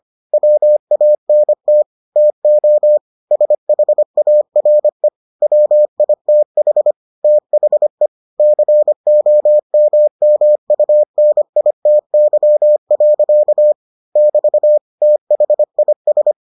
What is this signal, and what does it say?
Morse code, heard as a keyed tone, about 25 wpm, 600 Hz. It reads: WANT TO SHARE WITH THE COMMUNITY. <BT> THIS